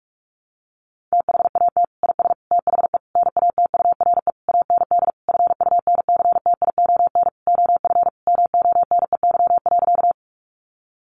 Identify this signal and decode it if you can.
Morse code, heard as a keyed tone, about 44 wpm, 700 Hz. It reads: THAT IS THE NATURE AND FUNCTION OF MONEY.
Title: Sentences from Top 1100 Words 45wpm